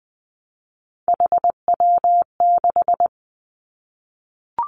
Morse code, heard as a keyed tone, about 19 wpm, 700 Hz. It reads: HW6
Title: Three Challenging Letter-Number Combinations 20wpm